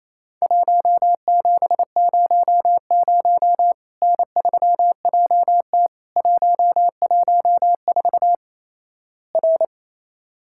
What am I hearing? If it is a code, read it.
Morse code, heard as a keyed tone, about 28 wpm, 700 Hz. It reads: 1700 N3JT 114 R
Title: Worked All Europe DX Contest - QTC Exchange - 28wpm